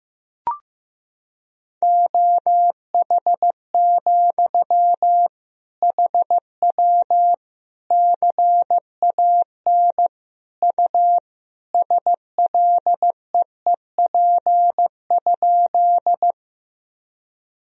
Morse code, heard as a keyed tone, about 15 wpm, 700 Hz. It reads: OH, HW CAN U SLEEP?
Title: Sentences from Top 500 Words and Abbreviations 15wpm